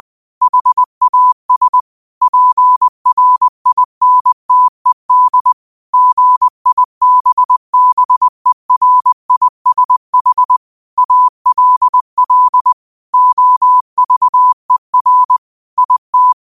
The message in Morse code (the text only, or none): HAS PRINTED GIBBERISH ALL OVER IT